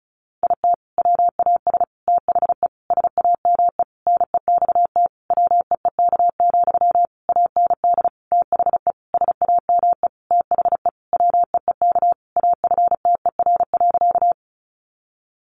Morse code, heard as a keyed tone, about 34 wpm, 700 Hz. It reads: IT WAS THE SAME NEXT WEEK, AND THE SAME THE WEEK AFTER.
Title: Sentences from Top 500 Words 35wpm